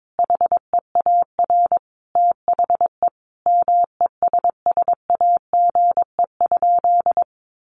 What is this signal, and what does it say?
Morse code, heard as a keyed tone, about 22 wpm, 700 Hz. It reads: HEAR THE MESSAGE?
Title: Sentences from Top 1100 Words 22wpm